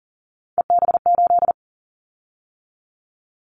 Morse code, heard as a keyed tone, about 40 wpm, 700 Hz. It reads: EB8
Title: Callsign Prefixes for Top 100 DXCC - 40wpm